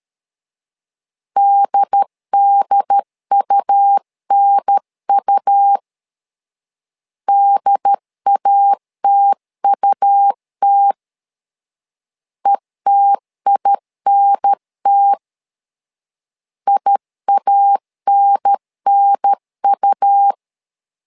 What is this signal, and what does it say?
Morse code, heard as a keyed tone, about 13 wpm, 800 Hz. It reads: DDUNU DATUT ETINT IANNU
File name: LEZIONI / 2 UHDN / 2 UHDN 12wpm 1456